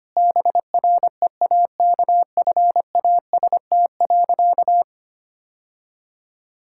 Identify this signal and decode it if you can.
Morse code, heard as a keyed tone, about 25 wpm, 700 Hz. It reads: BREAKFAST.
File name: Easy Sentences with Hints - 25wpm